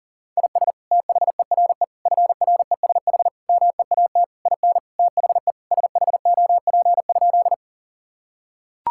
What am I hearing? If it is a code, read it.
Morse code, heard as a keyed tone, about 40 wpm, 700 Hz. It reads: IS THERE FRESH MEAT IN THE SHOP?
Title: Sentences from Top 1600 Words 40wpm